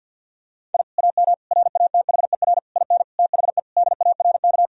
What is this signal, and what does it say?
Morse code, heard as a keyed tone, about 50 wpm, 700 Hz. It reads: I AM RATHER IN THE DARK
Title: Sentences from Top 1000 Words - Review Part 4 - 50wpm